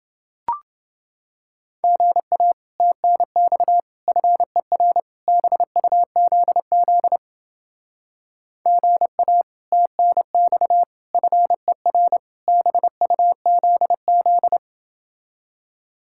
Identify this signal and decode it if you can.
Morse code, heard as a keyed tone, about 31 wpm, 700 Hz. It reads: GA TNX FER BUZZ GA TNX FER BUZZ